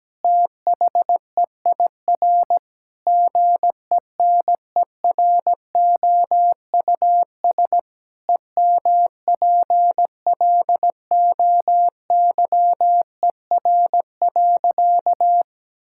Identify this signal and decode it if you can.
Morse code, heard as a keyed tone, about 17 wpm, 700 Hz. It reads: THEIR GENEROUS EMPLOYER.
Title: Pangram Challenge v2 - 17wpm